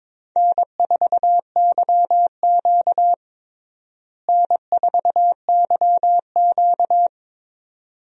Morse code, heard as a keyed tone, about 22 wpm, 700 Hz. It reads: N4YQ N4YQ
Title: USA Call Signs - 4 Characters - Repeated 2x - 22wpm